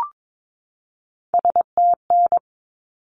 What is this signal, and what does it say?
Morse code, heard as a keyed tone, about 22 wpm, 700 Hz. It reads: STN
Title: Abbreviations 22wpm